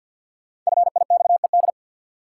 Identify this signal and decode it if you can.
Morse code, heard as a keyed tone, about 53 wpm, 700 Hz. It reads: FIXED
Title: Sentences from Top 1100 Words - New Word Drill Alt - 50wpm